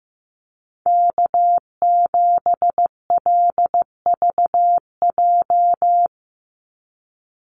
Morse code, heard as a keyed tone, about 15 wpm, 700 Hz. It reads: K7LVJ
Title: Call Signs Repeated 2x - 15wpm